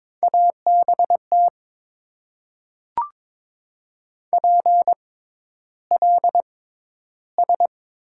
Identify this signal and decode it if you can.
Morse code, heard as a keyed tone, about 22 wpm, spaced as for 11 wpm, 700 Hz. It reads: ABT PLS
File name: Top 90 QSO Elements - 6x Character Spacing - 22wpm